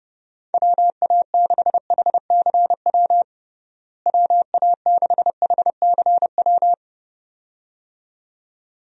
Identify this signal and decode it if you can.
Morse code, heard as a keyed tone, about 30 wpm, 700 Hz. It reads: WA6HCW WA6HCW
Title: USA Call Signs - 6 Characters - Repeated 2x - 30wpm